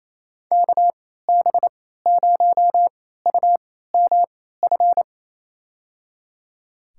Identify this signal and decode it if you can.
Morse code, heard as a keyed tone, about 28 wpm, spaced as for 16 wpm, 700 Hz. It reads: KB0UMF
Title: USA Call Signs - 6 Characters - 3x Character Spacing - 28wpm